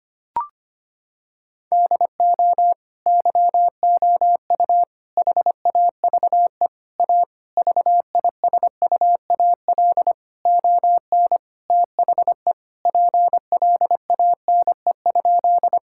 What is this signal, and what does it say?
Morse code, heard as a keyed tone, about 25 wpm, 700 Hz. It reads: DO YOU HAVE A VISUAL ON THE PLANE?
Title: Sentences from Top 1000 Words - Review Part 6 - 25wpm